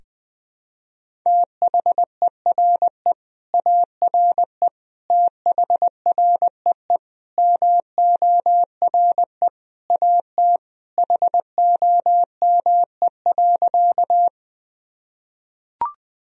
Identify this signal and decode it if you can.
Morse code, heard as a keyed tone, about 21 wpm, 700 Hz. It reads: THERE ARE THREE MORE AT HOME.
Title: Sentences from Top 1000 Words - Review Part 1 - 20wpm